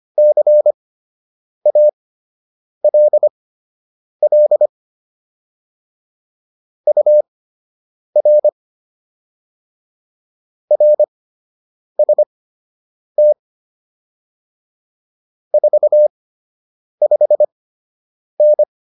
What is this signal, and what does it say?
Morse code, heard as a keyed tone, about 25 wpm, spaced as for 8 wpm, 600 Hz. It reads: CALL UR RST 45N